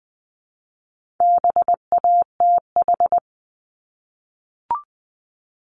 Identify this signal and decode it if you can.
Morse code, heard as a keyed tone, about 20 wpm, 700 Hz. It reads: BATH
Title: Multi-letter Phonemes Repeated 3x 20wpm